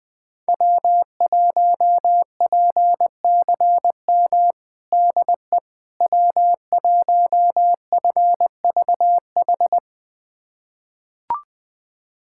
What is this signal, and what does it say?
Morse code, heard as a keyed tone, about 20 wpm, 700 Hz. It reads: W1PCM DE W1FVH
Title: USA Callsign DE Callsign - 5 Characters - 20wpm